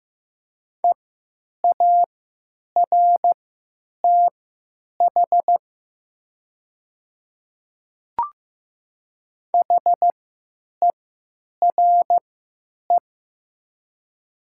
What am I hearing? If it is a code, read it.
Morse code, heard as a keyed tone, about 15 wpm, spaced as for 9 wpm, 700 Hz. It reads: EARTH HERE
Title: 3x Character Spacing - Top 200 Words - 15wpm